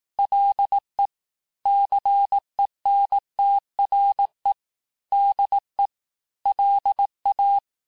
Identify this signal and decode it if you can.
Morse code, heard as a keyed tone, about 18 wpm, 800 Hz. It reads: LE CENTRE DE LA
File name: cours-du-lundi files / cours-1852-FIN-du-QSO-vitesse-17MPM